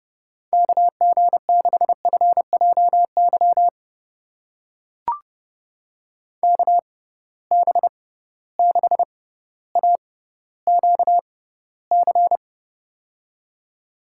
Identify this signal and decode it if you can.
Morse code, heard as a keyed tone, about 30 wpm, spaced as for 15 wpm, 700 Hz. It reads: KG6FJY KB6AQC
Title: USA Call Signs - 6 Characters - 6x Character Spacing - 30wpm